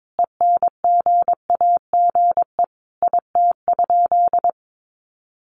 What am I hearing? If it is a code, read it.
Morse code, heard as a keyed tone, about 22 wpm, 700 Hz. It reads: ENGAGE IT?
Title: Sentences from Top 1500 Words 22wpm